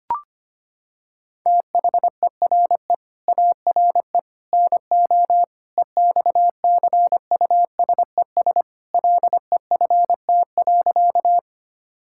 Morse code, heard as a keyed tone, about 25 wpm, 700 Hz. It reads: THERE ARE NO EXCUSES LEFT.
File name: practice / 25wpm / Sentences with 3-Word Phrases - 25wpm